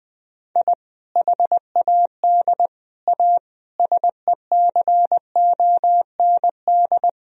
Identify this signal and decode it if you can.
Morse code, heard as a keyed tone, about 20 wpm, 700 Hz. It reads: I HAD A SECOND
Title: Sentences from Top 1000 Words - Review Part 1 - 20wpm